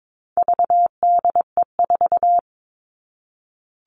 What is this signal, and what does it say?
Morse code, heard as a keyed tone, about 22 wpm, 700 Hz. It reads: VDE4
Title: All Cut Numbers - 4-Digit 22wpm